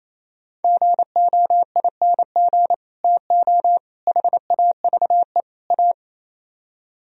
Morse code, heard as a keyed tone, about 36 wpm, 700 Hz. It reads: GOING TO HAVE A